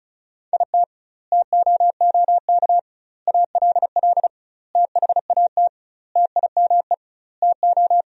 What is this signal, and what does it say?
Morse code, heard as a keyed tone, about 36 wpm, 700 Hz. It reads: IT TOOK ALL THAT TIME TO
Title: 2x Word-Spacing - Sentences from Top 300 Words - 35wpm